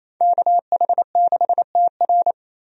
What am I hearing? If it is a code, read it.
Morse code, heard as a keyed tone, about 28 wpm, 700 Hz. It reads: KH6TR
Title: DXCC-100 Call Signs - 5 Characters - 28wpm